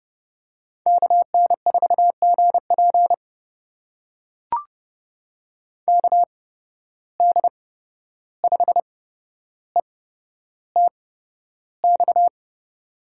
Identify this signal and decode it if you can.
Morse code, heard as a keyed tone, about 30 wpm, spaced as for 12 wpm, 700 Hz. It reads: KN4GP KD5ETX
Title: USA Call Signs - 8x Character Spacing - 30wpm